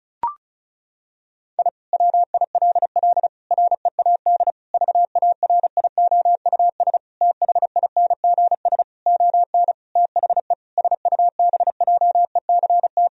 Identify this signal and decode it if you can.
Morse code, heard as a keyed tone, about 36 wpm, 700 Hz. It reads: I WILL READ VARIOUS THINGS ON THE SUBJECT
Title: Sentences from Top 1000 Words - Review Part 4 - 35wpm